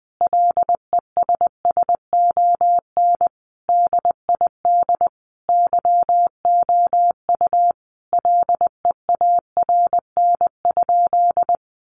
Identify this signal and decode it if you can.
Morse code, heard as a keyed tone, about 19 wpm, 700 Hz. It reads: LESSON DID YOU LEARN?
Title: Sentences from Top 1200 Words 20wpm